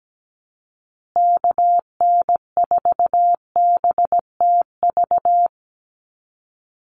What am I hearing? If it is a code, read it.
Morse code, heard as a keyed tone, about 17 wpm, 700 Hz. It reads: KN4BTV
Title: USA Call Signs Repeated 2x 17wpm